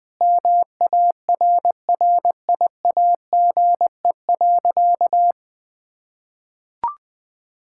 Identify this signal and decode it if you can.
Morse code, heard as a keyed tone, about 21 wpm, 700 Hz. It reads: MARRIAGE.
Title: Sentences from Top 1000 Words - Review Part 5 - 20wpm